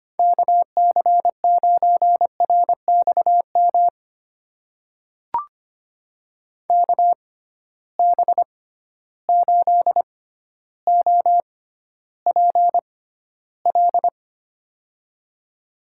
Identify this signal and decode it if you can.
Morse code, heard as a keyed tone, about 25 wpm, spaced as for 13 wpm, 700 Hz. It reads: KC9RXM KB8OPL
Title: USA Call Signs - 6 Characters - 6x Character Spacing - 25wpm